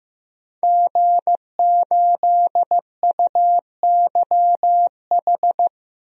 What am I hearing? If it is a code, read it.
Morse code, heard as a keyed tone, about 15 wpm, 700 Hz. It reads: G8UYH
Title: DXCC-100 Call Signs - 5 Characters - Repeated 3x - 15wpm